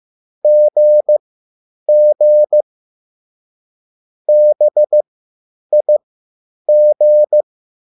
Morse code, heard as a keyed tone, about 15 wpm, spaced as for 9 wpm, 600 Hz. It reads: GG BIG
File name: QSL / CW lesson